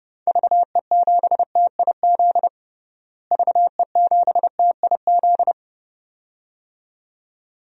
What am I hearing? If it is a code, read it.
Morse code, heard as a keyed tone, about 30 wpm, 700 Hz. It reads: VE7TIZ VE7TIZ
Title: DXCC-100 Call Signs - 6 Characters - Repeated 2x - 30wpm